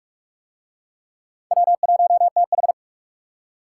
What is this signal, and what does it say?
Morse code, heard as a keyed tone, about 42 wpm, 700 Hz. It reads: W1TH